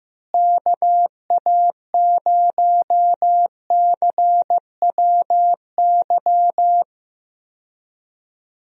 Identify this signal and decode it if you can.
Morse code, heard as a keyed tone, about 15 wpm, 700 Hz. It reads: KA0CWY
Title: USA Call Signs - 6 Characters - Repeated 3x - 15wpm